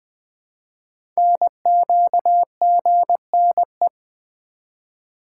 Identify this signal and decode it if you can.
Morse code, heard as a keyed tone, about 20 wpm, 700 Hz. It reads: NQGNE
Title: Code Groups - 5 Letters - 20wpm